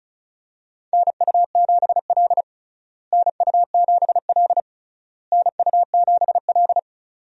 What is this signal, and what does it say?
Morse code, heard as a keyed tone, about 34 wpm, 700 Hz. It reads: NU7L NU7L NU7L
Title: USA Call Signs - 4 Characters - Repeated 3x - 35wpm